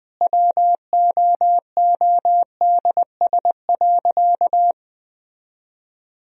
Morse code, heard as a keyed tone, about 21 wpm, 700 Hz. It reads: WOODS.
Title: Sentences from Top 1000 Words - Review Part 2 - 20wpm